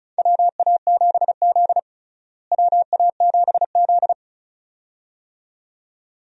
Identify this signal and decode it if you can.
Morse code, heard as a keyed tone, about 36 wpm, 700 Hz. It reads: WA7Z WA7Z
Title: USA Call Signs - 4 Characters - Repeated 2x - 35wpm